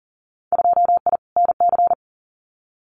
Morse code, heard as a keyed tone, about 40 wpm, 700 Hz. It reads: 2I NC
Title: Winter Field Day Exchanges 40wpm